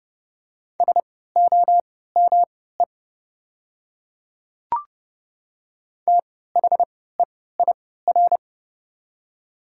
Morse code, heard as a keyed tone, about 30 wpm, spaced as for 17 wpm, 700 Hz. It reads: SOME THEIR
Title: 3x Character Spacing - Top 100 Words - 30wpm